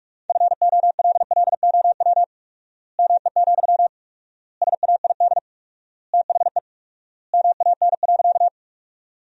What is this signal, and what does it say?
Morse code, heard as a keyed tone, about 46 wpm, 700 Hz. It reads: FOLLOW ME, SAID THE MAN.
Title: 4x Word-Spacing - Sentences from Top 200 Words - 45wpm